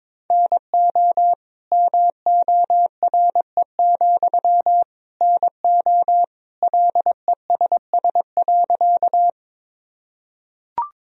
Morse code, heard as a keyed tone, about 22 wpm, 700 Hz. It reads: NO MORE, NO LESS.
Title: Sentences from Top 1000 Words - Review Part 2 - 22wpm